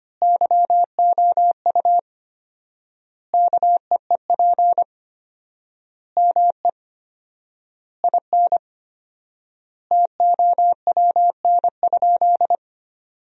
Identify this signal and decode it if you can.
Morse code, heard as a keyed tone, about 25 wpm, 700 Hz. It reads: YOU KEEP ME IN TOWN?
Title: 4x Word-Spacing - Sentences from Top 500 Words - 25wpm